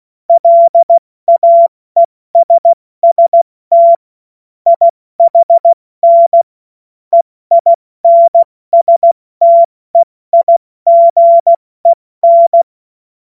Morse code, heard as a keyed tone, about 16 wpm, 680 Hz. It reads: LAESST IHN EINSTEIGEN
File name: Morsekurs / Witze / Witz 31. Mai 2022 70 F80